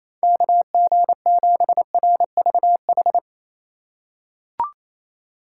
Morse code, heard as a keyed tone, about 28 wpm, 700 Hz. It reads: KG7RVH
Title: USA Call Signs - 6 Characters - 2x Character Spacing - 28wpm